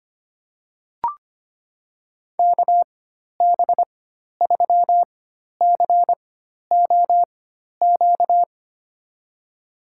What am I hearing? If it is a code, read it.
Morse code, heard as a keyed tone, about 25 wpm, spaced as for 12 wpm, 700 Hz. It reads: KB3COQ
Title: USA Call Signs - 4x Character Spacing - 25wpm